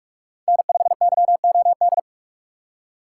Morse code, heard as a keyed tone, about 46 wpm, 700 Hz. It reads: N5YOD